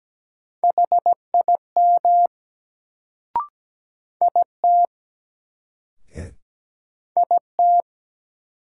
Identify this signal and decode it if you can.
Morse code, heard as a keyed tone, about 17 wpm, 700 Hz. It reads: HIM IT IT
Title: Sentences from Top 100 Words - New Word Drill - 17wpm